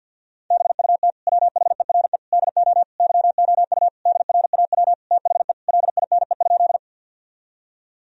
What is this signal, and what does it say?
Morse code, heard as a keyed tone, about 51 wpm, 700 Hz. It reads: BUT WHERE DO YOU DRAW THE LINE?
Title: Sentences from Top 1000 Words - Review Part 1 - 50wpm